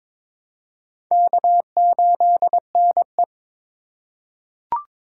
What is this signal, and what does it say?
Morse code, heard as a keyed tone, about 22 wpm, 700 Hz. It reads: K8NE
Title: USA Call Signs - 4 Characters - 4x Character Spacing - 22wpm